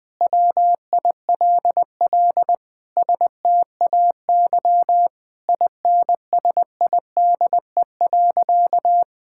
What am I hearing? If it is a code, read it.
Morse code, heard as a keyed tone, about 20 wpm, 700 Hz. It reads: WILL STAY INSIDE.
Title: Sentences with 2-Word Phrases - 20wpm